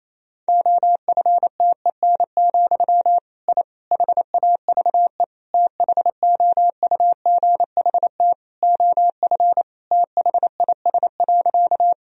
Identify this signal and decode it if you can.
Morse code, heard as a keyed tone, about 28 wpm, 700 Hz. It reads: OFTEN, I HAVE THOUGHT OF THIS.
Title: Sentences from Top 1000 Words - Review Part 1 - 28wpm